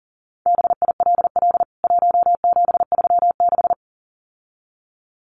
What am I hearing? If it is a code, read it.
Morse code, heard as a keyed tone, about 40 wpm, 700 Hz. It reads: BILL 1736
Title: CWT Exchange 40wpm